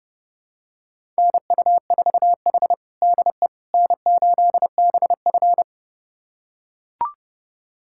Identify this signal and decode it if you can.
Morse code, heard as a keyed tone, about 30 wpm, 700 Hz. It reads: NU4H DE N8BF
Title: USA Callsign DE Callsign - 4 Characters - 30wpm